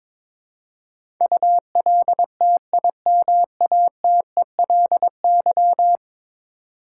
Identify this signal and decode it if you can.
Morse code, heard as a keyed tone, about 22 wpm, 700 Hz. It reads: ULTIMATELY
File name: practice / 22wpm / 10 Letter Words - First 3 Letters Given - 22wpm